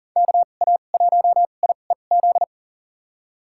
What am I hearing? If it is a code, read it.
Morse code, heard as a keyed tone, about 39 wpm, spaced as for 28 wpm, 700 Hz. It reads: KA1IEZ
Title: USA Call Signs - 6 Characters - 2x Character Spacing - 40wpm